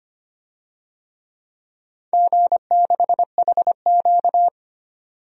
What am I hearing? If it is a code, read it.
Morse code, heard as a keyed tone, about 25 wpm, 700 Hz. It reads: G6HQ
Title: DXCC-100 Call Signs - 4 Characters - 25wpm